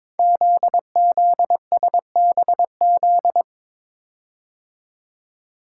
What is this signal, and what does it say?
Morse code, heard as a keyed tone, about 22 wpm, 700 Hz. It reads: ZZSBZ
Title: Code Groups - 5 Letters - 22wpm